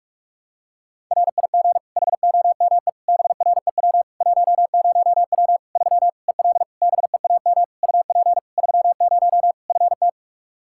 Morse code, heard as a keyed tone, about 46 wpm, 700 Hz. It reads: RIG HOME BREW 10W 3 EL BEAM UP 30 FT
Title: Practice QSO - Exchange 2 - Info - 45wpm